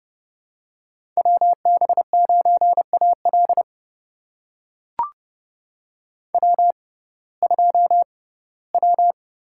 Morse code, heard as a keyed tone, about 30 wpm, spaced as for 18 wpm, 700 Hz. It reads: WB9AL W2W
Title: USA Call Signs - 6x Character Spacing - 30wpm